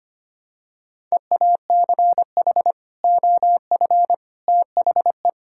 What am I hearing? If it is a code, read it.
Morse code, heard as a keyed tone, about 25 wpm, 700 Hz. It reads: EACH OF THE
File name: practice / 25wpm / Sentences with 3-Word Phrases - 25wpm